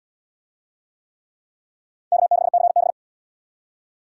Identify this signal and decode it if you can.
Morse code, heard as a keyed tone, about 75 wpm, 700 Hz. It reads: K6CB